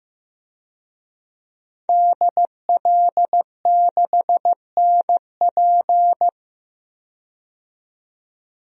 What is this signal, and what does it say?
Morse code, heard as a keyed tone, about 15 wpm, 700 Hz. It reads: DL6NP
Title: DXCC-100 Call Signs - 5 Characters - 15wpm